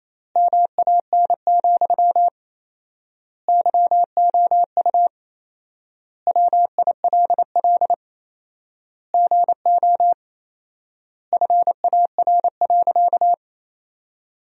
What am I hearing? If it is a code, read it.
Morse code, heard as a keyed tone, about 28 wpm, 700 Hz. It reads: MAN, YOU WILL GO FAR.
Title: 4x Word-Spacing - Sentences from Top 400 Words - 28wpm